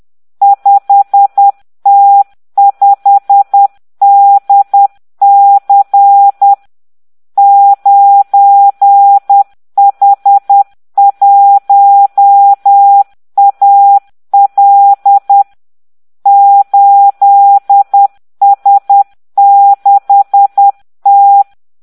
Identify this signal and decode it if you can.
Morse code, heard as a keyed tone, about 10 wpm, 800 Hz. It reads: 5T5DC 9H1AL 8S6T